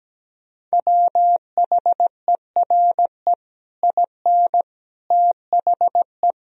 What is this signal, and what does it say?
Morse code, heard as a keyed tone, about 17 wpm, 700 Hz. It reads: WHERE IN THE
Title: Sentences from Top 200 Words and Abbreviations 17wpm